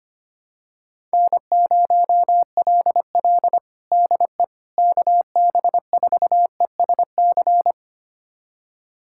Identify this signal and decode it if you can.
Morse code, heard as a keyed tone, about 25 wpm, 700 Hz. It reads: N0LL DE KB4ESC
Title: USA Callsign DE Callsign 25wpm